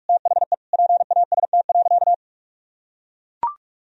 Morse code, heard as a keyed tone, about 41 wpm, 700 Hz. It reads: THE PAST.
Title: Sentences from Top 1000 Words - Review Part 3 - 45wpm